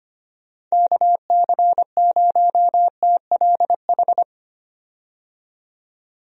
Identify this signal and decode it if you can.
Morse code, heard as a keyed tone, about 25 wpm, 700 Hz. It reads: KC0TLH
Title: USA Call Signs - 6 Characters - 25wpm